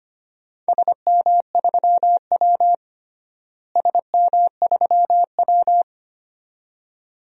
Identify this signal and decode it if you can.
Morse code, heard as a keyed tone, about 25 wpm, 700 Hz. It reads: SM3W SM3W
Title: DXCC-100 Call Signs - 4 Characters - Repeated 2x - 25wpm